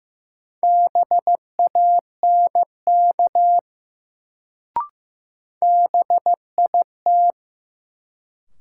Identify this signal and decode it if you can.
Morse code, heard as a keyed tone, about 15 wpm, 700 Hz. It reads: BANK BIT
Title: Sentences from Top 900 Words - New Word Drill - 15wpm